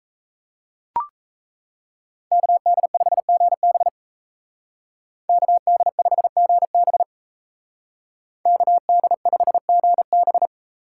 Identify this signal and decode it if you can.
Morse code, heard as a keyed tone, about 36 wpm, 700 Hz. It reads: KD5GB KD5GB KD5GB